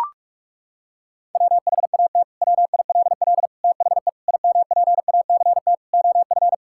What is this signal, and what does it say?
Morse code, heard as a keyed tone, about 41 wpm, 700 Hz. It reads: WHAT WILL THE IMPACT OF
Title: Sentences from Top 1000 Words - Review Part 6 - 45wpm